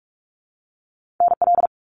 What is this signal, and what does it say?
Morse code, heard as a keyed tone, about 46 wpm, 700 Hz. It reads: NL
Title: Canadian Provinces and Territories - Repeated 3x - 45wpm